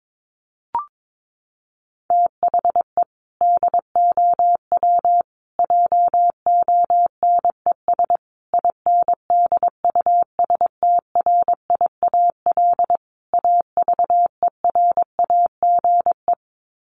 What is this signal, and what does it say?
Morse code, heard as a keyed tone, about 22 wpm, 700 Hz. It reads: THE DOW JONES INDUSTRIAL AVERAGE
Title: Sets of 5 Words 22wpm